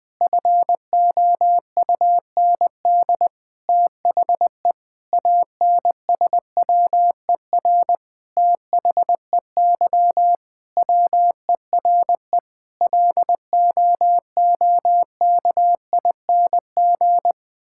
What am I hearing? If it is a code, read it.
Morse code, heard as a keyed tone, about 21 wpm, 700 Hz. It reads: FOUND THE ANSWER THEY WERE LOOKING
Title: Sentences from Top 1000 Words - Review Part 6 - 20wpm